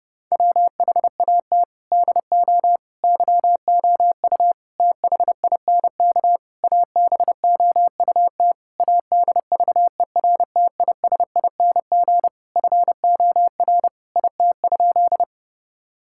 Morse code, heard as a keyed tone, about 30 wpm, 700 Hz. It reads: WHAT DO YOU THINK ABOUT ADVERTISING FOR IT?
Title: Sentences from Top 1000 Words - Review Part 6 - 30wpm